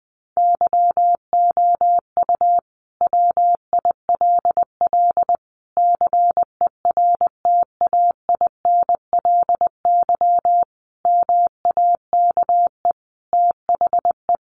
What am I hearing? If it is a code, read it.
Morse code, heard as a keyed tone, about 20 wpm, 700 Hz. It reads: YOU WILL CERTAINLY MAKE THE
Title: Sentences from Top 1400 Words 20wpm